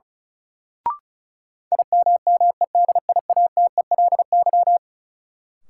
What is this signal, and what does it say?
Morse code, heard as a keyed tone, about 36 wpm, 700 Hz. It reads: IMMEDIATELY
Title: Sentences from Top 1100 Words - New Word Drill - 35wpm